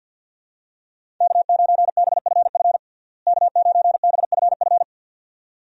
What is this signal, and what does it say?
Morse code, heard as a keyed tone, about 51 wpm, 700 Hz. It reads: K9BFF K9BFF
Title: USA Call Signs - 5 Characters - Repeated 2x - 50wpm